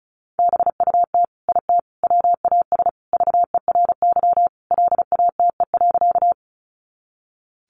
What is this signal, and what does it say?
Morse code, heard as a keyed tone, about 34 wpm, 700 Hz. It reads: BUT IT WAS VERY LATE.
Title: Sentences from Top 300 Words 35wpm